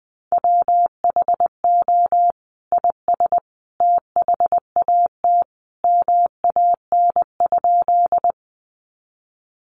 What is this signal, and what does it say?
Morse code, heard as a keyed tone, about 20 wpm, 700 Hz. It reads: WHO IS THAT MAN?
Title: Sentences from Top 200 Words 20wpm